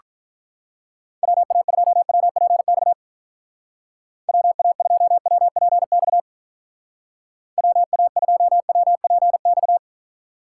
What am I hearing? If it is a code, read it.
Morse code, heard as a keyed tone, about 44 wpm, 700 Hz. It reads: WA2WPX WA2WPX WA2WPX
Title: Speed Racing - USA Call Signs 35wpm